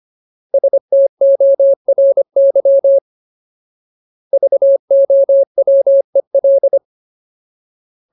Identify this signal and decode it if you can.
Morse code, heard as a keyed tone, about 25 wpm, 550 Hz. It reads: STORY VOWEL
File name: Ultimate Word List / 25WPM Ultimate